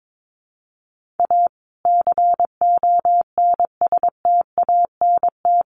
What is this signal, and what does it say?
Morse code, heard as a keyed tone, about 22 wpm, 700 Hz. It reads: A CONSTANT
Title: Sentences from Top 1500 Words 22wpm